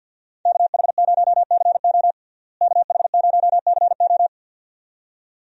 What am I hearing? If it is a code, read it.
Morse code, heard as a keyed tone, about 50 wpm, 700 Hz. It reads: KH0CO KH0CO
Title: USA Call Signs - 5 Characters - Repeated 2x - 50wpm